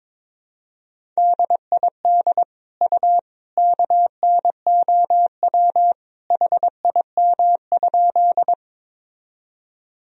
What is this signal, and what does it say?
Morse code, heard as a keyed tone, about 22 wpm, 700 Hz. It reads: DID U KNOW HIM?
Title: Sentences from Top 100 Words and Abbreviations 22wpm